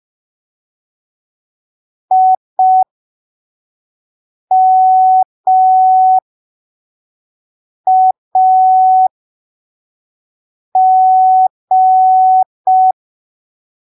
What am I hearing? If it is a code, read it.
Morse code, heard as a keyed tone, about 5 wpm, spaced as for 3 wpm, 750 Hz. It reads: IMAG
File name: arquivos / texto / texto 26 05ppm